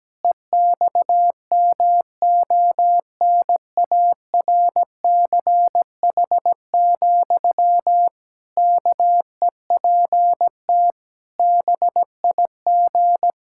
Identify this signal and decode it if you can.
Morse code, heard as a keyed tone, about 17 wpm, 700 Hz. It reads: EXMONARCH, KEPT BIG
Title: Pangram Challenge v2 - 17wpm